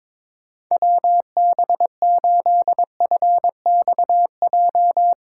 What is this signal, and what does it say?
Morse code, heard as a keyed tone, about 22 wpm, 700 Hz. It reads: WB8FXJ
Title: K1USN SST QSO Practice - 22wpm